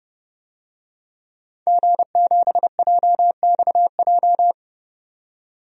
Morse code, heard as a keyed tone, about 30 wpm, 700 Hz. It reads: G7JXJ
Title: DXCC-100 Call Signs - 5 Characters - 30wpm